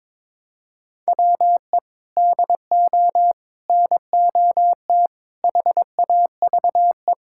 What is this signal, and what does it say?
Morse code, heard as a keyed tone, about 22 wpm, 700 Hz. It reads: WE DO NOT HAVE
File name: practice / 22wpm / Build A Phrase - 5-Word Phrases - 22wpm